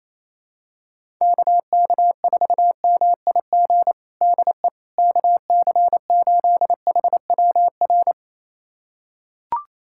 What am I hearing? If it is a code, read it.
Morse code, heard as a keyed tone, about 28 wpm, 700 Hz. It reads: KK4MIG DE KC8HWR
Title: USA Callsign DE Callsign 28wpm